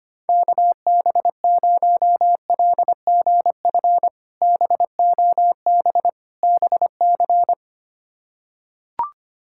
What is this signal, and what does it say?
Morse code, heard as a keyed tone, about 25 wpm, 700 Hz. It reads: KB0LGF BOB BC
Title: K1USN SST QSO Practice - 25wpm